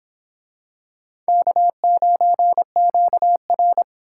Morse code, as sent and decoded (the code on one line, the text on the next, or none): -.- ----. --.- .-.
K9QR